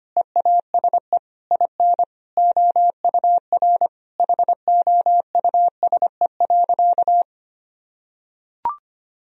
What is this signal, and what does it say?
Morse code, heard as a keyed tone, about 25 wpm, 700 Hz. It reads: EASE IN OUR HOUSE.
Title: Sentences from Top 1000 Words - Review Part 1 - 25wpm